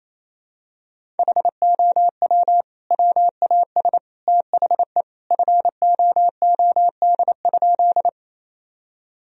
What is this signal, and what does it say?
Morse code, heard as a keyed tone, about 28 wpm, 700 Hz. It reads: HOW WAS THE FOOD?
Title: Sentences from Top 1000 Words - Review Part 1 - 28wpm